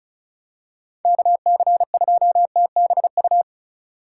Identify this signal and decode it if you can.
Morse code, heard as a keyed tone, about 36 wpm, 700 Hz. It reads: KC2TBU
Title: USA Call Signs - 6 Characters - 4x Character Spacing - 35wpm